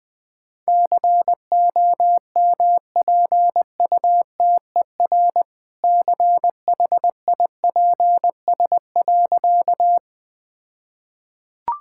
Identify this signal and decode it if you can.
Morse code, heard as a keyed tone, about 20 wpm, 700 Hz. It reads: COMPUTER CHIPS.
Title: All-Age Appropriate Jokes - 20wpm